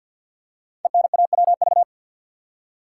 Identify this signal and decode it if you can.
Morse code, heard as a keyed tone, about 48 wpm, 700 Hz. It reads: ENAWV
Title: Code Groups - 5 Letters - 50wpm